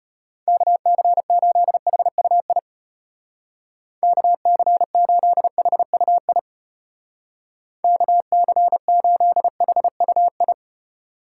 Morse code, heard as a keyed tone, about 30 wpm, 700 Hz. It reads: KC8HUI KC8HUI KC8HUI
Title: Speed Racing - USA 2x3 Call Signs 25wpm